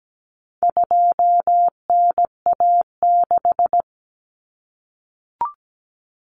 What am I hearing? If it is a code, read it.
Morse code, heard as a keyed tone, about 17 wpm, 700 Hz. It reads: 2NA6
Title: Common Cut Numbers - 4-Digit 17wpm